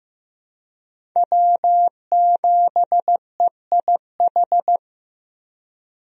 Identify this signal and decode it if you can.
Morse code, heard as a keyed tone, about 15 wpm, 700 Hz. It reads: W7EIH
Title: USA Call Signs - 5 Characters - Repeated 2x - 15wpm